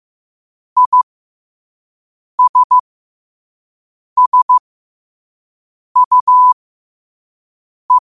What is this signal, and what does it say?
Morse code, heard as a keyed tone, about 15 wpm, spaced as for 5 wpm, 1000 Hz. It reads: ISSUE
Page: New quotes every day in morse code at 5 Words per minute.